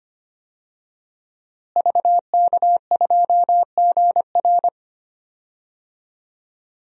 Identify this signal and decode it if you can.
Morse code, heard as a keyed tone, about 25 wpm, 700 Hz. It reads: VK2GR
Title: DXCC-100 Call Signs - 5 Characters - Repeated 3x - 25wpm